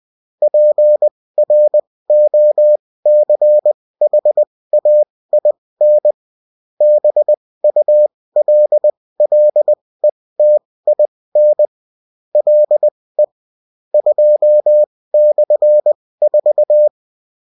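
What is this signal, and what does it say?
Morse code, heard as a keyed tone, about 20 wpm, spaced as for 16 wpm, 600 Hz. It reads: PROCHAIN BULLETIN LE 2/4
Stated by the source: Morse code bulletin of a French amateur CW union, sent as text and in CW.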